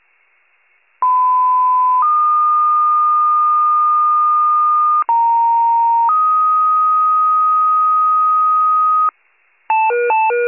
Two-tone page, heard as a one-second tone, then a three-second tone, about 1010 then 1220 Hz, about 930 then 1250 Hz. It is heard from 1.0 to 9.1 s.
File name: wcalert